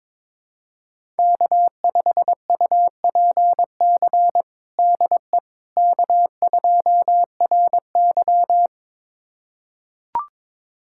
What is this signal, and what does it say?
Morse code, heard as a keyed tone, about 22 wpm, 700 Hz. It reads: K5UPC DE K2RY
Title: USA Callsign DE Callsign 22wpm